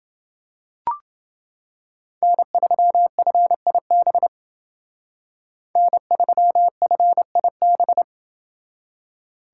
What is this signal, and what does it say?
Morse code, heard as a keyed tone, about 30 wpm, 700 Hz. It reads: N3FIB N3FIB